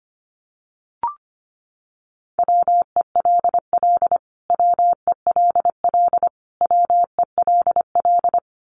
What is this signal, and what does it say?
Morse code, heard as a keyed tone, about 25 wpm, 700 Hz. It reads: WELL WELL WELL
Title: Top 200 Words Repeated 25wpm